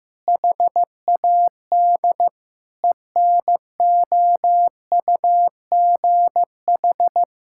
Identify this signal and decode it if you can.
Morse code, heard as a keyed tone, about 15 wpm, 700 Hz. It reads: HAD ENOUGH
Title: Sentences from Top 1000 Words - Review Part 5 - 15wpm